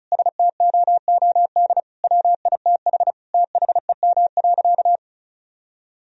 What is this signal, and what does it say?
Morse code, heard as a keyed tone, about 36 wpm, 700 Hz. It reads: STOOD WITH THEM.
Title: Sentences from Top 1000 Words - Review Part 2 - 35wpm